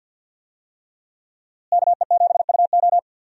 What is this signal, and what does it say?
Morse code, heard as a keyed tone, about 47 wpm, 700 Hz. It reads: KE7UO
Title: USA Call Signs - 5 Characters - Repeated 3x - 50wpm